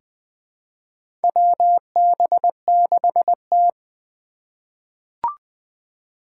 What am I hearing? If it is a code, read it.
Morse code, heard as a keyed tone, about 20 wpm, 700 Hz. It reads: WB6T
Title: USA Call Signs - 3x Character Spacing - 20wpm